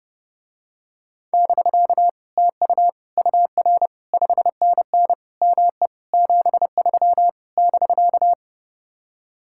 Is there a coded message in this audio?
Yes: Morse code from 1.3 to 8.3 s, heard as a keyed tone, about 30 wpm, 700 Hz. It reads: <BK> TU UR 5NN ME 73 <BK>